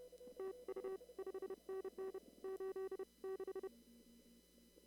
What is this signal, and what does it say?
Morse code, heard as a keyed tone, about 30 wpm, 380 Hz. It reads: TU 5NN 86
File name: CW / kt5j wpx cw-op-k5pi